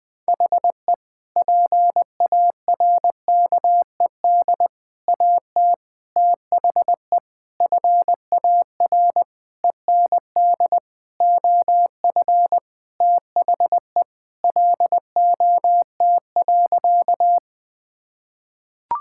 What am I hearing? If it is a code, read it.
Morse code, heard as a keyed tone, about 19 wpm, 700 Hz. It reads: HE PARKED AT THE FAR END OF THE LOT.
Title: Sentences with 5-Word Phrases - 20wpm